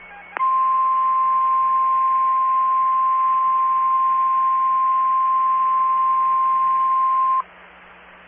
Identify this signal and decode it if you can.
Emergency Alert System (SAME) tones (the 1050 Hz attention tone). It is heard from 0.4 to 7.5 s.
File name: RR Skywarn / SWIRA Linked Repeaters and SKYWARN Spotters